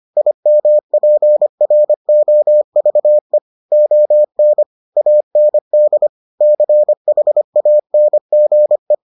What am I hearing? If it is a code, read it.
Morse code, heard as a keyed tone, about 25 wpm, 600 Hz. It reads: IMPROVE ON AND CHANGE